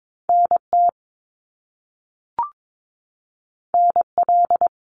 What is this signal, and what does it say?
Morse code, heard as a keyed tone, about 22 wpm, 700 Hz. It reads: NT NL
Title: Canadian Provinces and Territories 22wpm